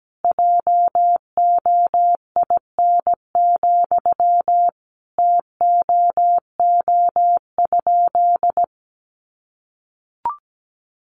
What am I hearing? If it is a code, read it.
Morse code, heard as a keyed tone, about 17 wpm, 700 Hz. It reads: JOIN, TOO?
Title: Sentences from Top 1400 Words 17wpm